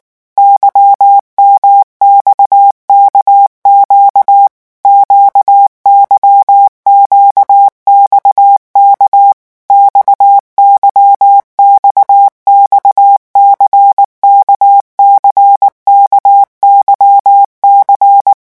Morse code, heard as a keyed tone, about 19 wpm, 800 Hz. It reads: YMXKQ QYQXK XYXXC KCKYC
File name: revision1-vitesse 18 mots